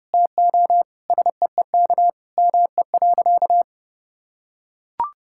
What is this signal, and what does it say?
Morse code, heard as a keyed tone, about 30 wpm, 700 Hz. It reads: TO SEEK ME.
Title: Sentences from Top 1600 Words 30wpm